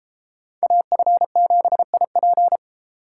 Morse code, heard as a keyed tone, about 29 wpm, 700 Hz. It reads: AF7IP